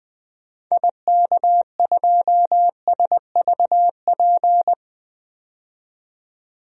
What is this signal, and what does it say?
Morse code, heard as a keyed tone, about 20 wpm, 700 Hz. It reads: IK2SVP
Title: DXCC-100 Call Signs - 6 Characters - 20wpm